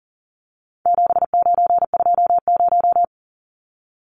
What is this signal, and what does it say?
Morse code, heard as a keyed tone, about 40 wpm, 700 Hz. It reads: 7920
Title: Numbers - 4-Digit 40wpm